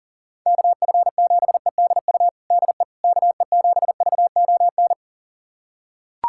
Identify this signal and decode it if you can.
Morse code, heard as a keyed tone, about 40 wpm, 700 Hz. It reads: KF7EDU DE KE7VON
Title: USA Callsign DE Callsign - 6 Characters - 40wpm